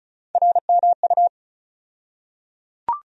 Morse code, heard as a keyed tone, about 34 wpm, 700 Hz. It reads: RMU
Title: USA Call Sign Postfixes - 3 Letters 35wpm